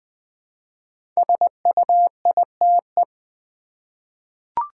Morse code, heard as a keyed tone, about 20 wpm, 700 Hz. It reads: SUITE
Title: 5 Letter Words - First 3 Letters Given - 20wpm